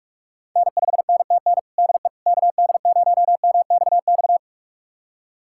Morse code, heard as a keyed tone, about 47 wpm, 700 Hz. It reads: N5NTN DE KD0MXX
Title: USA Callsign DE Callsign 45wpm